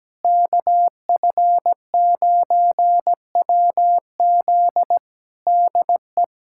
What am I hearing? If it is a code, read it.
Morse code, heard as a keyed tone, about 17 wpm, 700 Hz. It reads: KF9WZ DE
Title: K1USN SST QSO Practice 17wpm